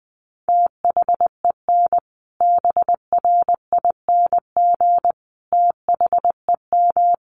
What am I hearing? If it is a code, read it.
Morse code, heard as a keyed tone, about 20 wpm, 700 Hz. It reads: THEN BRING THEM
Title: Sentences from Top 1500 Words 20wpm